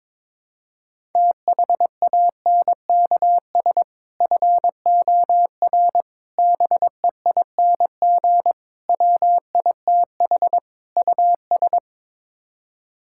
Morse code, heard as a keyed tone, about 22 wpm, 700 Hz. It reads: THANKS FOR BEING WITH US
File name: practice / 22wpm / Build A Phrase - 5-Word Phrases - 22wpm